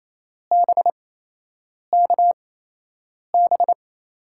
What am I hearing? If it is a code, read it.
Morse code, heard as a keyed tone, about 28 wpm, spaced as for 8 wpm, 700 Hz. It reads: BKB
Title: USA Call Signs - 8x Character Spacing - 28wpm